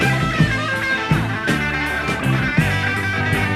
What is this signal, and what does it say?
Emergency Alert System (SAME) tones, heard as harsh buzzing bursts. It is heard from 0.6 to 3.2 s.